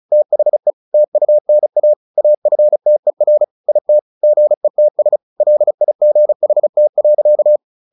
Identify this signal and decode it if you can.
Morse code, heard as a keyed tone, about 34 wpm, 600 Hz. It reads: THE TUNA AFTER IT GETS LIGHT.
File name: the old man and the sea / the old man and the sea 1 35wpm